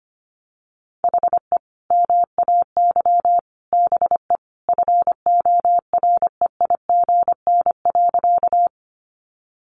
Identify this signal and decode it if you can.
Morse code, heard as a keyed tone, about 25 wpm, 700 Hz. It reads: HE MAY BE FOREIGN.
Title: Sentences from Top 1400 Words 25wpm